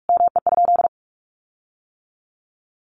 Morse code, heard as a keyed tone, about 44 wpm, 700 Hz. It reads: ME?
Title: Sentences from Top 1400 Words 45wpm